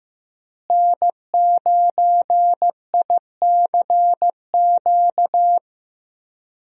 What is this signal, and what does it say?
Morse code, heard as a keyed tone, about 15 wpm, 700 Hz. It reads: N9ICQ
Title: USA Call Signs - 5 Characters - Repeated 2x - 15wpm